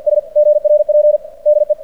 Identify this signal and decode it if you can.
Morse code, heard as a keyed tone, about 45 wpm, 600 Hz. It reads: S GRP DE
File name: archived / CRSnet